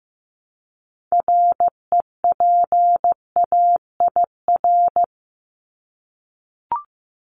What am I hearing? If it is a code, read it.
Morse code, heard as a keyed tone, about 15 wpm, 700 Hz. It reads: REPAIR
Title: Top 1501-1750 Words 15wpm